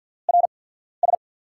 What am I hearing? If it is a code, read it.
Morse code, heard as a keyed tone, about 49 wpm, spaced as for 14 wpm, 700 Hz. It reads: RS
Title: USA Call Signs - 5 Characters - 8x Character Spacing - 50wpm